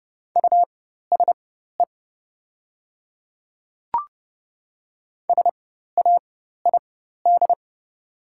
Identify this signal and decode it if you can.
Morse code, heard as a keyed tone, about 30 wpm, spaced as for 14 wpm, 700 Hz. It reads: USE SAID
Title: 4x Character Spacing - Top 100 Words - 30wpm